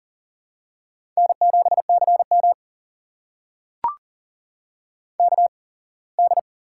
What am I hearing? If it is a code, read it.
Morse code, heard as a keyed tone, about 40 wpm, spaced as for 24 wpm, 700 Hz. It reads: N7CM KD
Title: USA Call Signs - 4 Characters - 8x Character Spacing - 40wpm